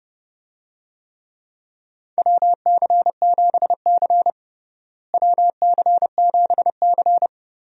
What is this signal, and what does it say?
Morse code, heard as a keyed tone, about 32 wpm, 700 Hz. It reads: WC7C WC7C
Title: USA Call Signs - 4 Characters - Repeated 3x - 30wpm